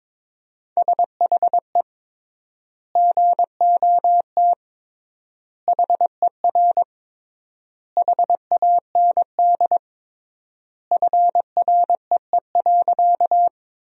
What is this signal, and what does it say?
Morse code, heard as a keyed tone, about 22 wpm, 700 Hz. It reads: SHE GOT HER HAND FREE.
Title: 3x Word-Spacing - Sentences from Top 500 Words - 22wpm